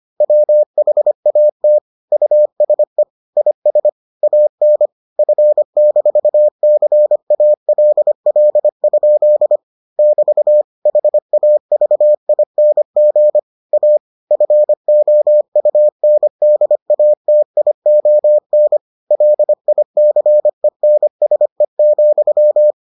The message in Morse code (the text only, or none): WHAT USE IS AN F-CALL? <BT> HAVING A FOUNDATION LICENSE,